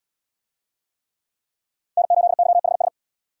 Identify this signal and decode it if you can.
Morse code, heard as a keyed tone, about 67 wpm, 700 Hz. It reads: N8YHS